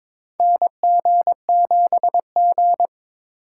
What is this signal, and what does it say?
Morse code, heard as a keyed tone, about 22 wpm, 700 Hz. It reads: NG7G
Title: USA Call Signs - 4x Character Spacing - 22wpm